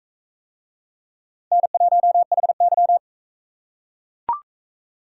Morse code, heard as a keyed tone, about 38 wpm, 700 Hz. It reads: N1HY